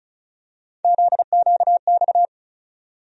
Morse code, heard as a keyed tone, about 36 wpm, 700 Hz. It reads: ZQX
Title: USA Call Sign Postfixes - 3 Letters 35wpm